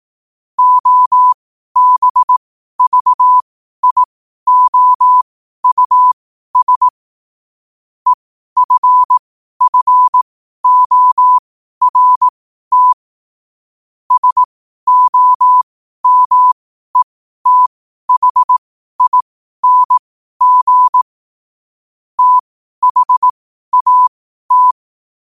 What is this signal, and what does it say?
Morse code, heard as a keyed tone, about 18 wpm, spaced as for 12 wpm, 1000 Hz. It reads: OBVIOUS EFFORT SOMETHING THAT
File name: qotdfiles / 12wpm